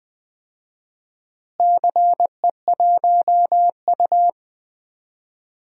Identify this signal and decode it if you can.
Morse code, heard as a keyed tone, about 20 wpm, 700 Hz. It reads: CE1U
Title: DXCC-100 Call Signs - 4 Characters - Repeated 3x - 20wpm